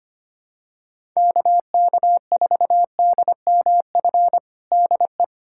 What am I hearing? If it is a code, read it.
Morse code, heard as a keyed tone, about 25 wpm, 700 Hz. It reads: KK4DMF DE
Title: USA Callsign DE Callsign 25wpm